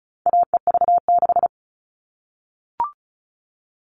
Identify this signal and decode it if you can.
Morse code, heard as a keyed tone, about 34 wpm, 700 Hz. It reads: AEV6
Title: All Cut Numbers - 4-Digit 35wpm